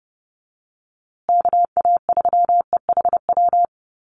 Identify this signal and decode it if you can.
Morse code, heard as a keyed tone, about 30 wpm, 700 Hz. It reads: KA3EHW
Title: USA Call Signs Repeated 3x 30wpm